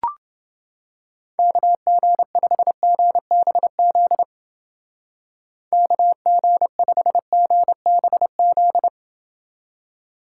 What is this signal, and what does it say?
Morse code, heard as a keyed tone, about 29 wpm, 700 Hz. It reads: KG5GBZ KG5GBZ